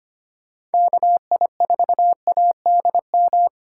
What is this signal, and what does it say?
Morse code, heard as a keyed tone, about 32 wpm, 700 Hz. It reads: KI4ADM